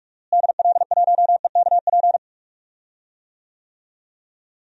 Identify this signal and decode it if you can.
Morse code, heard as a keyed tone, about 46 wpm, 700 Hz. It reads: DL1EKP
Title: DXCC-100 Call Signs - 6 Characters - Repeated 2x - 45wpm